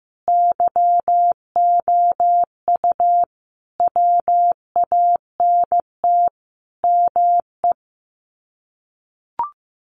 Morse code, heard as a keyed tone, about 15 wpm, 700 Hz. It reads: YOU WANT ME
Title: Sets of 5 Words Encore2 15wpm